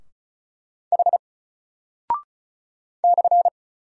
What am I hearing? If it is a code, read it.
Morse code, heard as a keyed tone, about 36 wpm, 700 Hz. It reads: H /
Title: Single Character (Letter:Number:Punctuation:Prosign) Alt 35wpm